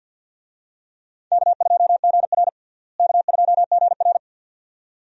Morse code, heard as a keyed tone, about 47 wpm, 700 Hz. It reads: K2GR K2GR
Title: USA Call Signs - 4 Characters - Repeated 2x - 50wpm